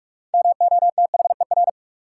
Morse code, heard as a keyed tone, about 44 wpm, 700 Hz. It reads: MOTHER
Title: Sentences from Top 200 Words - New Word Drill - 45wpm